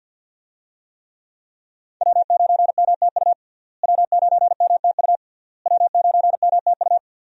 Morse code, heard as a keyed tone, about 50 wpm, 700 Hz. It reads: W9MTU W9MTU W9MTU
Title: USA Call Signs - 5 Characters - Repeated 3x - 50wpm